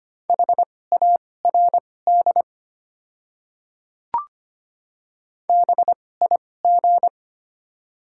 Morse code, heard as a keyed tone, about 25 wpm, spaced as for 18 wpm, 690 Hz. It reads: HARD BIG
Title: 2x Character Spacing - Top 300 Words - 25wpm